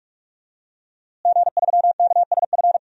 Morse code, heard as a keyed tone, about 44 wpm, 700 Hz. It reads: G3KSF
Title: DXCC-100 Call Signs - 5 Characters - Repeated 3x - 45wpm